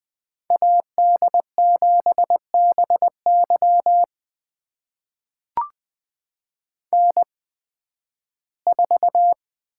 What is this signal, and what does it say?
Morse code, heard as a keyed tone, about 20 wpm, spaced as for 13 wpm, 700 Hz. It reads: AD7BY N4
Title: USA Call Signs - 5 Characters - 8x Character Spacing - 20wpm